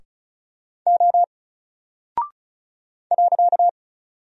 Morse code, heard as a keyed tone, about 36 wpm, 700 Hz. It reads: O .
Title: Single Character (Letter:Number:Punctuation:Prosign) Alt 35wpm